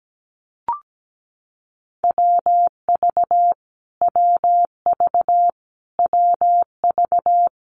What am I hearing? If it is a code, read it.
Morse code, heard as a keyed tone, about 17 wpm, 700 Hz. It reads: WV WV WV
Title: US States Repeated 3x 17wpm